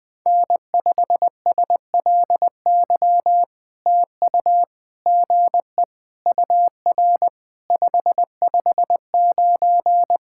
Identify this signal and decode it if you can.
Morse code, heard as a keyed tone, about 20 wpm, 700 Hz. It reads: N5SLY TU GE UR 559
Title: POTA QSO Practice - 20wpm